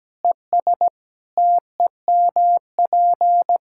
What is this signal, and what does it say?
Morse code, heard as a keyed tone, about 17 wpm, 700 Hz. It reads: ES TEMP
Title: Practice QSO - Exchange 2 - Info - 17wpm